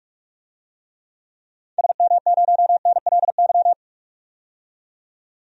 Morse code, heard as a keyed tone, about 46 wpm, 700 Hz. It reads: SM0NLY
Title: DXCC-100 Call Signs - 6 Characters - 45wpm